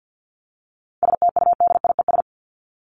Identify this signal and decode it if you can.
Morse code, heard as a keyed tone, about 50 wpm, 700 Hz. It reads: STUDIES
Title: ES:S Suffix - Top 100 Words with Base Words 50wpm